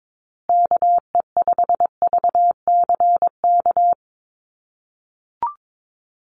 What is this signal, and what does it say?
Morse code, heard as a keyed tone, about 22 wpm, 700 Hz. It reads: KE5VCK
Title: USA Call Signs Repeated 2x 22wpm